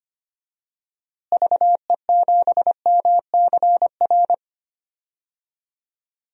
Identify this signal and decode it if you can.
Morse code, heard as a keyed tone, about 25 wpm, 700 Hz. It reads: VE7MCR
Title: DXCC-100 Call Signs - 6 Characters - Repeated 3x - 25wpm